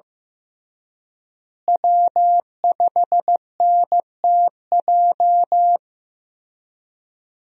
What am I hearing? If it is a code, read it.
Morse code, heard as a keyed tone, about 15 wpm, 700 Hz. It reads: W5NTJ
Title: USA Call Signs - 5 Characters - Repeated 3x - 15wpm